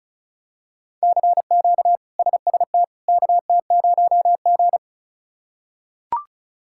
Morse code, heard as a keyed tone, about 36 wpm, 700 Hz. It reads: CQ SST KT0G
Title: K1USN SST QSO Practice - 35wpm